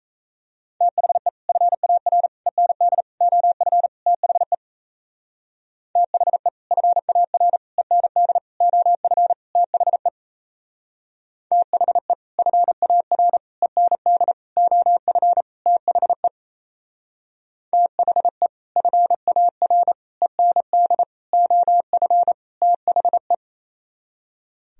Morse code, heard as a keyed tone, about 33 wpm, 700 Hz. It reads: THE FAR END OF THE THE FAR END OF THE THE FAR END OF THE THE FAR END OF THE
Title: Speed-Racing - Sets of 5 Words 28wpm